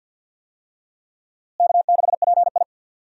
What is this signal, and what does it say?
Morse code, heard as a keyed tone, about 51 wpm, 690 Hz. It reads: K6PI
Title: USA Call Signs - 4 Characters - Repeated 2x - 50wpm